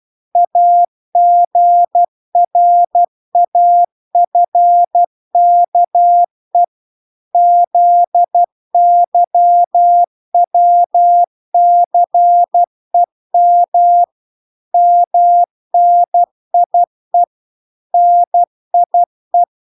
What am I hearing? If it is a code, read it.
Morse code, heard as a keyed tone, about 12 wpm, 700 Hz. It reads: AGRAFKE ZYWCEM MNIE NIE